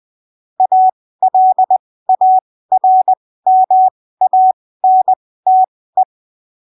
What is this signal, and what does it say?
Morse code, heard as a keyed tone, about 20 wpm, spaced as for 15 wpm, 750 Hz. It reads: ALARMANTE
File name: arquivos / texto / texto 24 20ppm